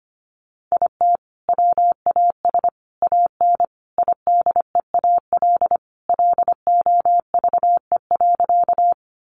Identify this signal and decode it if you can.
Morse code, heard as a keyed tone, about 25 wpm, 700 Hz. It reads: IT WAS AN IDEAL LOVE.
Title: Sentences from Top 1400 Words 25wpm